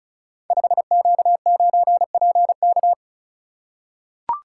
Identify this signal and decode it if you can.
Morse code, heard as a keyed tone, about 36 wpm, 690 Hz. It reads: 5Q9PK
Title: Code Groups - 5 Letter:Numbers - 35wpm